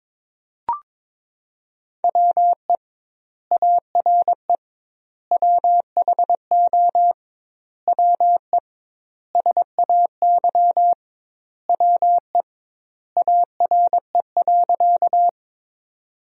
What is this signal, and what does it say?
Morse code, heard as a keyed tone, about 22 wpm, 700 Hz. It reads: WE ARE WHO WE SAY WE ARE.
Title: 2x Word-Spacing - Sentences from Top 200 Words - 22wpm